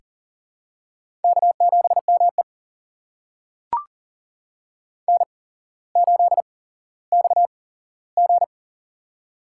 Morse code, heard as a keyed tone, about 40 wpm, spaced as for 17 wpm, 700 Hz. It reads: K7ME N8XG
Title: USA Call Signs - 4 Characters - 8x Character Spacing - 40wpm